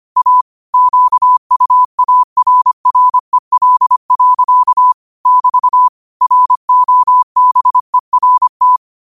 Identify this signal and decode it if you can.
Morse code, heard as a keyed tone, about 25 wpm, 1000 Hz. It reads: A QUARREL. <BT> ROBERT